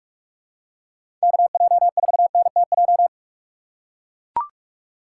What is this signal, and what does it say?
Morse code, heard as a keyed tone, about 44 wpm, 700 Hz. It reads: KJ4NTJ
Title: USA Call Signs - 6 Characters - 3x Character Spacing - 45wpm